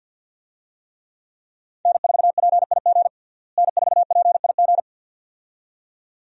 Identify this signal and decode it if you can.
Morse code, heard as a keyed tone, about 46 wpm, 700 Hz. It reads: N4PIG N4PIG
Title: USA Call Signs - 5 Characters - Repeated 2x - 50wpm